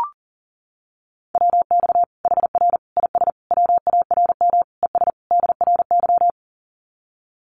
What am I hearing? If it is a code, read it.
Morse code, heard as a keyed tone, about 40 wpm, 700 Hz. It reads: WX HR IS WARM ES DRY
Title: Common QSO Phrases 40wpm